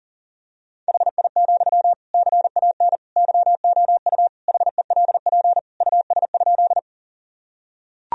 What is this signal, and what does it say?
Morse code, heard as a keyed tone, about 40 wpm, 700 Hz. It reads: HI, CAN YOU HELP US?
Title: Sentences from Top 1000 Words - Review Part 3 - 40wpm